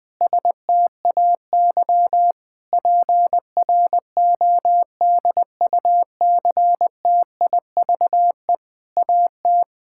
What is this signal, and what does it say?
Morse code, heard as a keyed tone, about 21 wpm, 700 Hz. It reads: STAY PRODUCTIVE AT
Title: Difficult Sentences with Hints - 20wpm